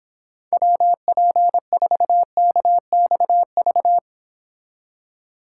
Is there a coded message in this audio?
Yes: Morse code from 0.5 to 4.0 s, heard as a keyed tone, about 26 wpm, 700 Hz. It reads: WP4KXV